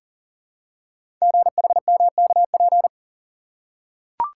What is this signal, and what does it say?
Morse code, heard as a keyed tone, about 43 wpm, 700 Hz. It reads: GHMKP
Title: Code Groups - 5 Letters - 40wpm